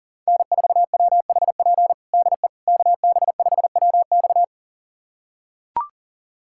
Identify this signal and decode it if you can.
Morse code, heard as a keyed tone, about 40 wpm, 700 Hz. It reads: N4WHP DE KB5WX
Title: USA Callsign DE Callsign - 5 Characters - 40wpm